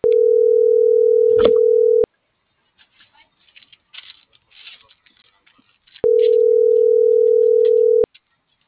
Ringback tone (North America), 440 + 480 Hz.